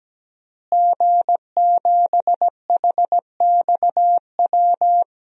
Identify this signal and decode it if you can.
Morse code, heard as a keyed tone, about 17 wpm, 700 Hz. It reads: G7HXW
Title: DXCC-100 Call Signs - 5 Characters - Repeated 2x - 17wpm